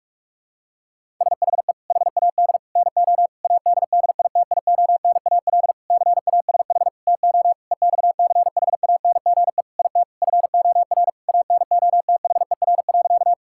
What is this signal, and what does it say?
Morse code, heard as a keyed tone, about 46 wpm, 700 Hz. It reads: SHE HAD NO ADDITIONAL CASH TO EXCHANGE IT FOR ANOTHER.
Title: Sentences from Top 1000 Words - Review Part 6 - 45wpm